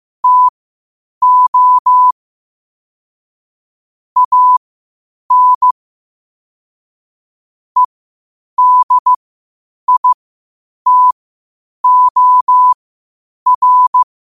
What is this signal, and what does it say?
Morse code, heard as a keyed tone, about 15 wpm, spaced as for 8 wpm, 1000 Hz. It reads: TO AN EDITOR
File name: qotdfiles / 8wpm